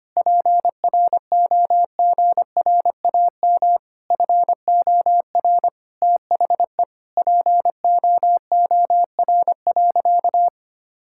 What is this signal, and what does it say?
Morse code, heard as a keyed tone, about 25 wpm, 700 Hz. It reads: PROGRAM FOR THE POOR.
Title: Sentences from Top 1000 Words - Review Part 5 - 25wpm